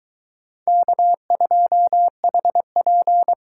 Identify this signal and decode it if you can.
Morse code, heard as a keyed tone, about 22 wpm, 700 Hz. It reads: K2HP